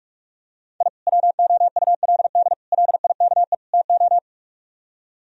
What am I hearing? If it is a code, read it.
Morse code, heard as a keyed tone, about 47 wpm, 700 Hz. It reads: I WOULD LIKE TO